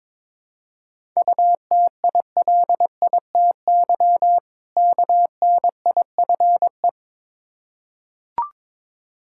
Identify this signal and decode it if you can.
Morse code, heard as a keyed tone, about 21 wpm, 690 Hz. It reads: UTILITY KNIFE
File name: Difficult Words with Hints - 22wpm